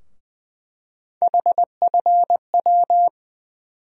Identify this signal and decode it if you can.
Morse code, heard as a keyed tone, about 20 wpm, 700 Hz. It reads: HFW
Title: USA Call Sign Postfixes - 3 Letters 20wpm